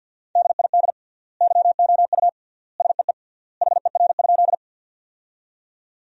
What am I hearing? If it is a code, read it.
Morse code, heard as a keyed tone, about 52 wpm, 700 Hz. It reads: DID YOU SEE HER?
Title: 3x Word-Spacing - Sentences from Top 100 Words - 50wpm